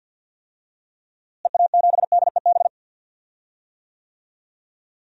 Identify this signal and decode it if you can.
Morse code, heard as a keyed tone, about 46 wpm, 700 Hz. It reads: EA7DEB
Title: DXCC-100 Call Signs - 6 Characters - Repeated 2x - 50wpm